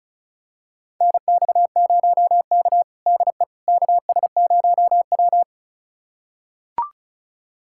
Morse code, heard as a keyed tone, about 36 wpm, 700 Hz. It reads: NX0K DE KS0W
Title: USA Callsign DE Callsign - 4 Characters - 35wpm